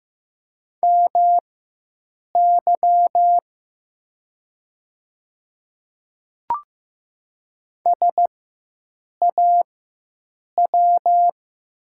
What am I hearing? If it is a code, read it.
Morse code, heard as a keyed tone, about 15 wpm, spaced as for 7 wpm, 700 Hz. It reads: MY SAW
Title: 4x Character Spacing - Top 300 Words - 15wpm